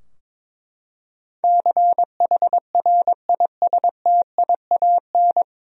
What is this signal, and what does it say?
Morse code, heard as a keyed tone, about 22 wpm, 700 Hz. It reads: CHRISTIAN
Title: USA Common Names - 22wpm